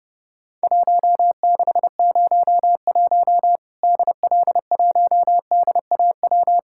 Morse code, heard as a keyed tone, about 30 wpm, 700 Hz. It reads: 1601 DL1DAW
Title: Worked All Europe DX Contest - QTC Exchange - 30wpm